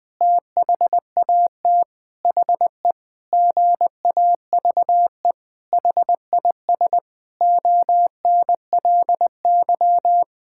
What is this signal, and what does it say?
Morse code, heard as a keyed tone, about 20 wpm, 700 Hz. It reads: THAT HE GAVE HIS ONLY
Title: Sentences with 8-Word Phrases - 20wpm